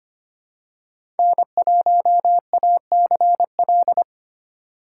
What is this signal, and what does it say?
Morse code, heard as a keyed tone, about 25 wpm, 700 Hz. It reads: N1ACL
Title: USA Call Signs - 5 Characters - 4x Character Spacing - 25wpm